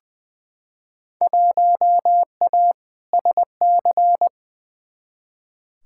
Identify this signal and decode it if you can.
Morse code, heard as a keyed tone, about 21 wpm, 700 Hz. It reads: 1A SC
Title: Field Day Exchanges - 20wpm